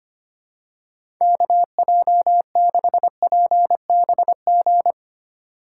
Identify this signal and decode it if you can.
Morse code, heard as a keyed tone, about 25 wpm, 700 Hz. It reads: KJ6PBG
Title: USA Call Signs - 6 Characters - 4x Character Spacing - 25wpm